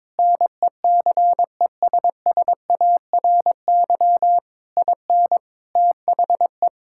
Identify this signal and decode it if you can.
Morse code, heard as a keyed tone, about 22 wpm, 700 Hz. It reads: NECESSARY IN THE
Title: Sentences from Top 1000 Words - Review Part 5 - 22wpm